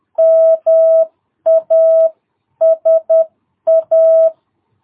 Morse code, heard as a keyed tone, about 10 wpm, 660 Hz. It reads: MASA
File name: jh8cbh / 2022cw